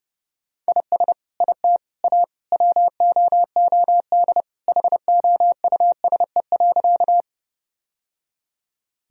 Morse code, heard as a keyed tone, about 30 wpm, 700 Hz. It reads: IS IT A WOOD HOUSE.
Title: Sentences from Top 1000 Words - Review Part 2 - 30wpm